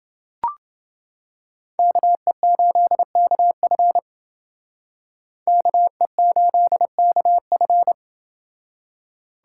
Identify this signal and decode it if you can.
Morse code, heard as a keyed tone, about 31 wpm, 700 Hz. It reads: KE8KF KE8KF